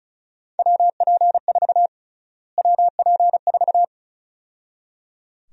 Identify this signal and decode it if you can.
Morse code, heard as a keyed tone, about 34 wpm, 700 Hz. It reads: WP4 WP4
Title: USA Call Sign Prefixes - Repeated 2x - 35wpm